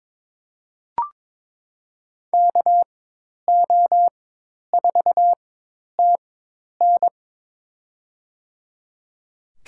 Morse code, heard as a keyed tone, about 22 wpm, spaced as for 10 wpm, 700 Hz. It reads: KO4TN
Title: USA Call Signs - 5 Characters - 4x Character Spacing - 22wpm